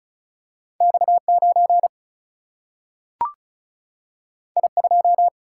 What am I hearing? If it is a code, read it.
Morse code, heard as a keyed tone, about 36 wpm, 700 Hz. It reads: X9 I2
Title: Single-Letter Single-Number 35wpm